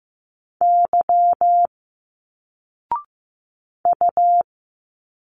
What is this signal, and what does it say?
Morse code, heard as a keyed tone, about 15 wpm, 700 Hz. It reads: Y U
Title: Single Letters 15wpm